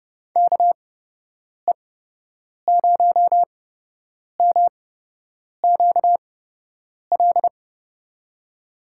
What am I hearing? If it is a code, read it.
Morse code, heard as a keyed tone, about 31 wpm, spaced as for 8 wpm, 700 Hz. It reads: KE0MQL
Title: USA Call Signs - 8x Character Spacing - 30wpm